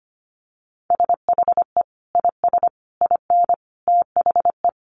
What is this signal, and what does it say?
Morse code, heard as a keyed tone, about 25 wpm, 700 Hz. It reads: SHE IS IN THE
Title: Sentences from Top 1200 Words 25wpm